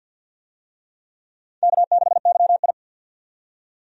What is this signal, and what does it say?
Morse code, heard as a keyed tone, about 47 wpm, 700 Hz. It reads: K6YI
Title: USA Call Signs - 4 Characters - 50wpm